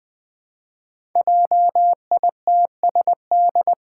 Morse code, heard as a keyed tone, about 20 wpm, 700 Hz. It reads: JITSD
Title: Five Challenging Letter-Number Combinations 20wpm